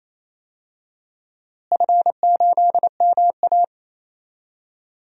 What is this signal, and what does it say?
Morse code, heard as a keyed tone, about 28 wpm, 700 Hz. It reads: F8MA
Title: DXCC-100 Call Signs - 4 Characters - Repeated 2x - 28wpm